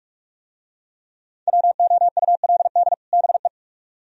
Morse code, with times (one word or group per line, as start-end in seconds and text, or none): 1.5-2.9 WOULD
3.1-3.5 BE